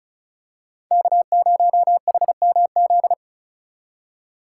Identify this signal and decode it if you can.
Morse code, heard as a keyed tone, about 34 wpm, 700 Hz. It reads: K0HMZ
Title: K1USN SST QSO Practice - 35wpm